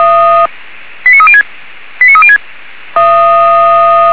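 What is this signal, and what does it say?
Five-tone selective call (ZVEI), heard as quick runs of five tones, tones of about 70 ms. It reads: 89306 89306
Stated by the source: ZVEI five-tone selective call 89288 received by SDR.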